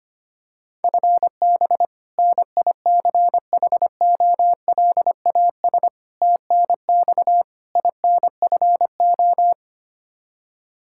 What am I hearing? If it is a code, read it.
Morse code, heard as a keyed tone, about 25 wpm, 700 Hz. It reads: FB NICHOLAS TNX INFO
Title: K1USN SST QSO Practice 25wpm